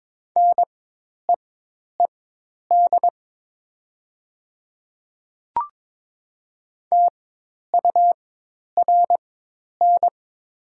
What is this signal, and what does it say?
Morse code, heard as a keyed tone, about 22 wpm, spaced as for 10 wpm, 700 Hz. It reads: NEED TURN
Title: 4x Character Spacing - Top 300 Words - 22wpm